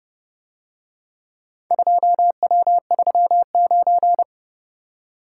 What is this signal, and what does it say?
Morse code, heard as a keyed tone, about 30 wpm, 700 Hz. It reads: 2W39